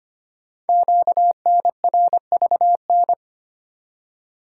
Morse code, heard as a keyed tone, about 25 wpm, 700 Hz. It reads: QNRVN
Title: Code Groups - 5 Letters - 25wpm